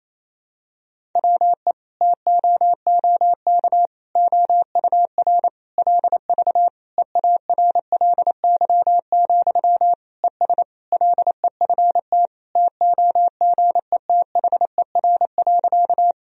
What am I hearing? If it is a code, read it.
Morse code, heard as a keyed tone, about 28 wpm, 700 Hz. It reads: WE TOOK OUR LV EARLY, ES LEFT TOGETHER.
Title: Sentences from Top 400 Words and Abbreviations 28wpm